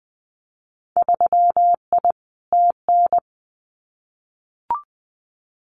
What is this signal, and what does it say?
Morse code, heard as a keyed tone, about 20 wpm, 700 Hz. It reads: 3I TN
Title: Winter Field Day Exchanges 20wpm